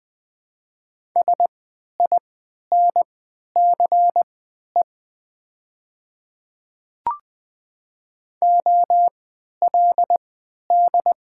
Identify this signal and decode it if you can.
Morse code, heard as a keyed tone, about 20 wpm, spaced as for 11 wpm, 700 Hz. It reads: SINCE OLD
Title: 3x Character Spacing - Top 300 Words - 20wpm